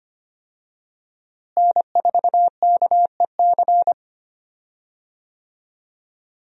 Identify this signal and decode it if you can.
Morse code, heard as a keyed tone, about 25 wpm, 700 Hz. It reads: N4KEC
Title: USA Call Signs - 5 Characters - Repeated 3x - 25wpm